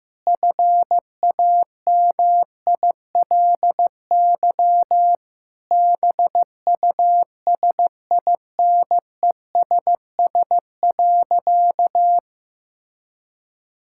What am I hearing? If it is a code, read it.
Morse code, heard as a keyed tone, about 16 wpm, 700 Hz. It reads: FAMILY BUSINESS.
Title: Sentences from Top 1000 Words - Review Part 6 - 15wpm